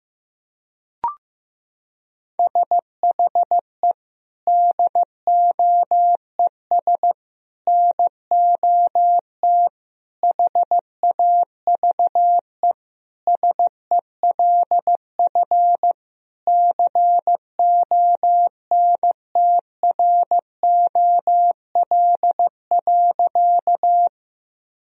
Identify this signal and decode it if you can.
Morse code, heard as a keyed tone, about 15 wpm, 700 Hz. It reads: SHE DOES NOT HAVE SELF CONTROL.
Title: Sentences from Top 1000 Words - Review Part 3 - 15wpm